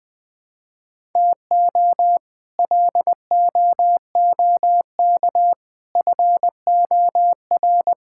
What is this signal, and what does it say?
Morse code, heard as a keyed tone, about 20 wpm, 700 Hz. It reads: TO LOOK FOR
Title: Sets of 6 Words - 20wpm